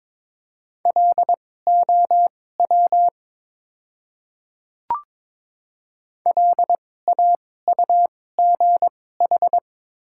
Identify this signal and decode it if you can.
Morse code, heard as a keyed tone, about 22 wpm, spaced as for 16 wpm, 700 Hz. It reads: LOW LAUGH
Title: 2x Character Spacing - Top 500 Words - 22wpm